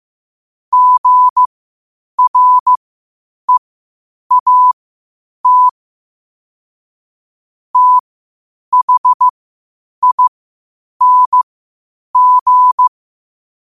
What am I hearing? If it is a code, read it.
Morse code, heard as a keyed tone, about 15 wpm, spaced as for 8 wpm, 1000 Hz. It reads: GREAT THING
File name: qotdfiles / 8wpm